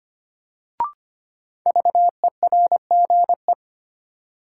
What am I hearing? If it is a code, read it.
Morse code, heard as a keyed tone, about 25 wpm, 700 Hz. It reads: VERGE
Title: Sets of 4 Words - New Word Drill - 25wpm